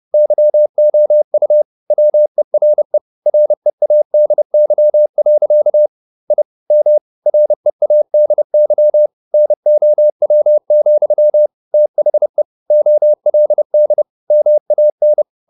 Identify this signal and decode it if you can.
Morse code, heard as a keyed tone, about 30 wpm, 600 Hz. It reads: YOU WERE READY. I M READY NOW, THE OLD MAN
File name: the old man and the sea / the old man and the sea 1 30wpm